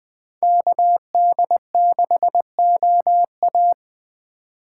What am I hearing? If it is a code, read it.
Morse code, heard as a keyed tone, about 21 wpm, 700 Hz. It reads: KD6OA